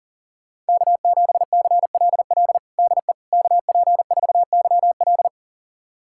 Morse code, heard as a keyed tone, about 40 wpm, 700 Hz. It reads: K7CLL DE KP4YL
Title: USA Callsign DE Callsign - 5 Characters - 40wpm